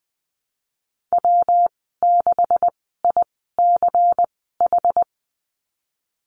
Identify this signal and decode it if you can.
Morse code, heard as a keyed tone, about 20 wpm, spaced as for 15 wpm, 700 Hz. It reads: W6ICH
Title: 128 Lesson 21g - Number 6 - Call Signs 20wpm